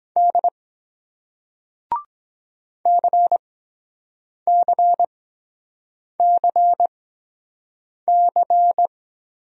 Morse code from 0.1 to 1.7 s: D

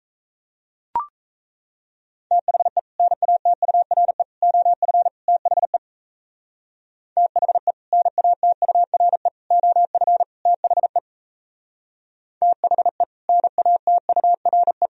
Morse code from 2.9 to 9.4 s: NATURE OF THE THE NATURE

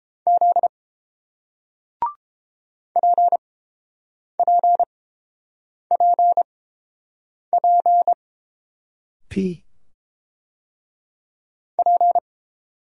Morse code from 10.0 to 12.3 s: P

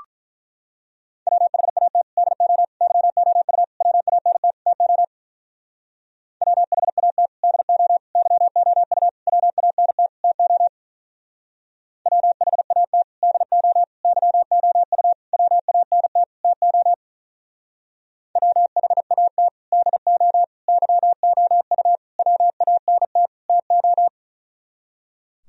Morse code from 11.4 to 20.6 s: WHAT DO YOU WANT TO WHAT DO

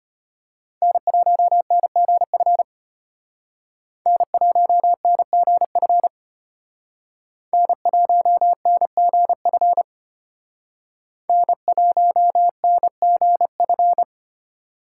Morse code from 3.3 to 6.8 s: N1NGF

